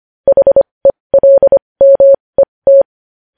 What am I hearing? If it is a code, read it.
Morse code, heard as a keyed tone, about 25 wpm, spaced as for 20 wpm, 550 Hz. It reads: HELMET
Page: news-headlines-in-morse-code-at-20-wpm